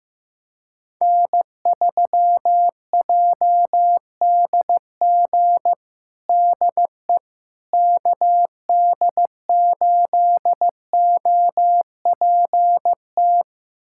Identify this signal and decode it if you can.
Morse code, heard as a keyed tone, about 15 wpm, 700 Hz. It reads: N3JDG DE KD8OPT
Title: USA Callsign DE Callsign 15wpm